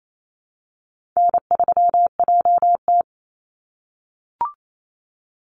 Morse code, heard as a keyed tone, about 28 wpm, 700 Hz. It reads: N3JT
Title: Easy Call Signs - 28wpm